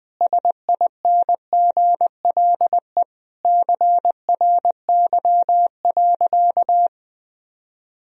Morse code, heard as a keyed tone, about 20 wpm, 700 Hz. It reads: SINGLE CRY.
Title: Sentences from Top 1000 Words - Review Part 4 - 20wpm